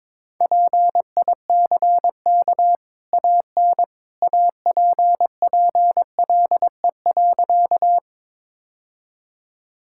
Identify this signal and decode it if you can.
Morse code, heard as a keyed tone, about 22 wpm, 690 Hz. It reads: PICK AN APPLE.
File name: practice / 22wpm / Easy Sentences with Hints - 22wpm